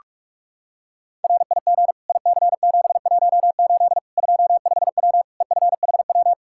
Morse code, heard as a keyed tone, about 42 wpm, 700 Hz. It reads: RIG IC718 25W EFHW
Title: Practice QSO - Exchange 2 - Info - 45wpm